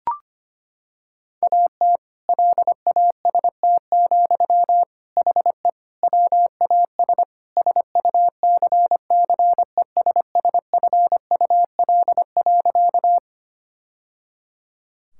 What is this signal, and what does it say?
Morse code, heard as a keyed tone, about 25 wpm, 700 Hz. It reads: AT LAST, HE WAS SUCCESSFUL.
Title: Sentences from Top 1000 Words - Review Part 6 - 25wpm